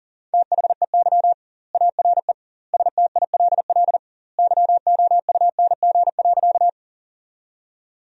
Morse code, heard as a keyed tone, about 37 wpm, 700 Hz. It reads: THEY ARE STILL YOUNG.
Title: 2x Word-Spacing - Sentences from Top 400 Words - 40wpm